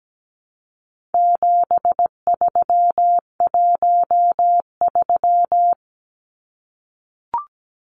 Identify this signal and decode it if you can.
Morse code, heard as a keyed tone, about 17 wpm, 700 Hz. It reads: 7313
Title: Numbers - 4-Digit 17wpm